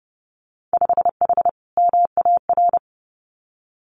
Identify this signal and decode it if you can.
Morse code, heard as a keyed tone, about 30 wpm, 700 Hz. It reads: 5H MAR
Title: Winter Field Day Exchanges Encore 30wpm